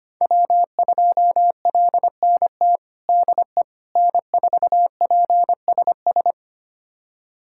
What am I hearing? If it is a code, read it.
Morse code, heard as a keyed tone, about 25 wpm, 700 Hz. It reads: W2LNT DE N4PSS
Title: USA Callsign DE Callsign - 5 Characters - 25wpm